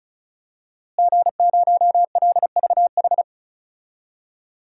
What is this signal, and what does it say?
Morse code, heard as a keyed tone, about 34 wpm, 700 Hz. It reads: G0LVH
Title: DXCC-100 Call Signs - 5 Characters - Repeated 2x - 35wpm